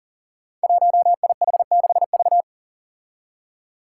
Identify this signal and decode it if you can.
Morse code, heard as a keyed tone, about 40 wpm, 700 Hz. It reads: 1IH6V
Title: Five Challenging Letter-Number Combinations 40wpm